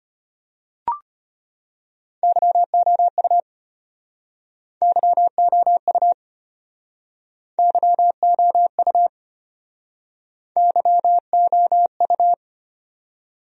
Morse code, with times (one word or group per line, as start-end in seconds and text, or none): 2.2-3.4 YOU
4.8-6.1 YOU
7.6-9.1 YOU
10.6-12.3 YOU